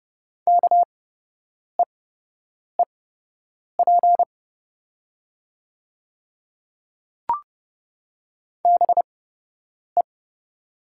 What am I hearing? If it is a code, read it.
Morse code, heard as a keyed tone, about 30 wpm, spaced as for 8 wpm, 700 Hz. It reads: KEEP BE
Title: 8x Character Spacing - Top 300 Words - 30wpm